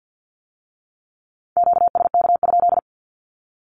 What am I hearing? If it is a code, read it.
Morse code, heard as a keyed tone, about 50 wpm, 700 Hz. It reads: QSK?
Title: Q-Codes with Spoken Meaning - 50wpm